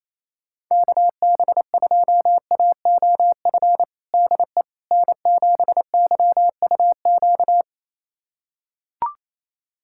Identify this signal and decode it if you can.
Morse code, heard as a keyed tone, about 28 wpm, 700 Hz. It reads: KB2AOF DE N7YUQ
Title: USA Callsign DE Callsign 28wpm